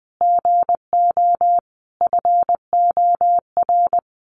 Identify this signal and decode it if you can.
Morse code, heard as a keyed tone, about 20 wpm, 700 Hz. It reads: GO FOR
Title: Sentences from Top 1000 Words 20wpm